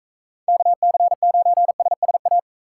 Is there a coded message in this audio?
Yes: Morse code from 0.5 to 2.4 s, heard as a keyed tone, about 44 wpm, 700 Hz. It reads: KC9SSA